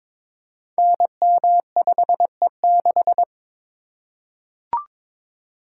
Morse code, heard as a keyed tone, about 22 wpm, 700 Hz. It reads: NM5E6
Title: Code Groups - 5 Letter:Numbers - 22wpm